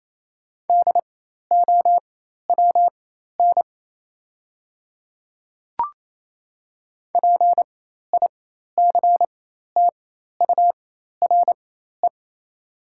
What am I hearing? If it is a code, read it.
Morse code, heard as a keyed tone, about 28 wpm, spaced as for 13 wpm, 690 Hz. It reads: DOWN PICTURE
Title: 4x Character Spacing - Top 300 Words - 28wpm